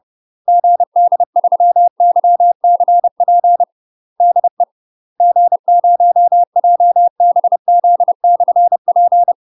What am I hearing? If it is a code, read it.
Morse code, heard as a keyed tone, about 30 wpm, 700 Hz. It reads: GD3YCP DE G0JBZ/P
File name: UK MASTER QSOs TRAINING / 30WPM / UK MASTER QSOs 0026